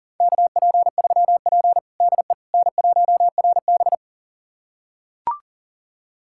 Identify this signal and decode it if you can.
Morse code, heard as a keyed tone, about 37 wpm, 700 Hz. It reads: KP3P DE N1RB
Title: USA Callsign DE Callsign - 4 Characters - 40wpm